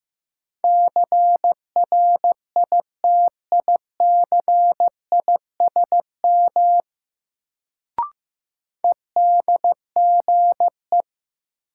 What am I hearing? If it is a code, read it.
Morse code, heard as a keyed tone, about 15 wpm, 700 Hz. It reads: CRITICISM EDGE
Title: Sentences from Top 1300 Words - New Word Drill Alt - 15wpm